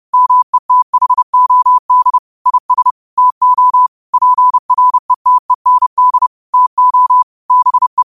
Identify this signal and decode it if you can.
Morse code, heard as a keyed tone, about 30 wpm, 1000 Hz. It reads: METHOD IS TO PRETEND TO BE